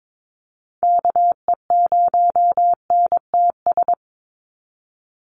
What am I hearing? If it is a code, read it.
Morse code, heard as a keyed tone, about 22 wpm, 700 Hz. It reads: KE0NTS
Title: USA Call Signs 22wpm